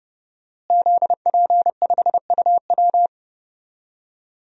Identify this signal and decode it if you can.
Morse code, heard as a keyed tone, about 32 wpm, 700 Hz. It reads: ZP5UW
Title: DXCC-100 Call Signs - 5 Characters - Repeated 2x - 30wpm